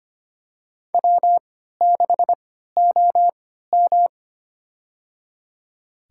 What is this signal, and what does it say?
Morse code, heard as a keyed tone, about 25 wpm, spaced as for 14 wpm, 700 Hz. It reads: W6OM
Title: USA Call Signs - 4 Characters - 3x Character Spacing - 25wpm